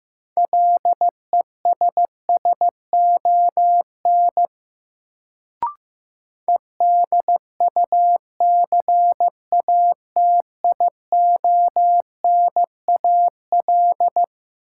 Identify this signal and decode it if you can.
Morse code, heard as a keyed tone, about 15 wpm, 700 Hz. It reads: LESSON EDUCATIONAL
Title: Sentences from Top 1200 Words - New Word Drill - 15wpm